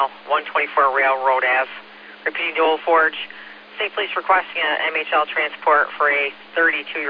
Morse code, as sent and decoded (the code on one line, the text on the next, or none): none